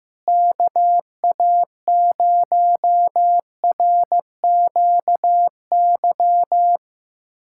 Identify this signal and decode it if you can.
Morse code, heard as a keyed tone, about 15 wpm, 700 Hz. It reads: KA0RQY
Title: USA Callsign DE Callsign - 6 Characters - 15wpm